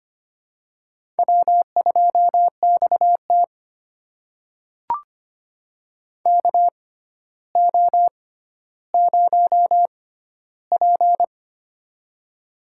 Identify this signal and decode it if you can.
Morse code, heard as a keyed tone, about 25 wpm, spaced as for 13 wpm, 700 Hz. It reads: W2XT KO0P
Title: USA Call Signs - 4 Characters - 6x Character Spacing - 25wpm